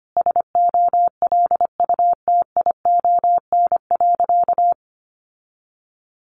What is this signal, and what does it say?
Morse code, heard as a keyed tone, about 25 wpm, 700 Hz. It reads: SOLUTION.
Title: Sentences from Top 1300 Words 25wpm